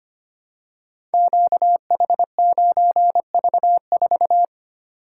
Morse code, heard as a keyed tone, about 25 wpm, 700 Hz. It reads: QH9V4
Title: Code Groups - 5 Letter:Numbers - 25wpm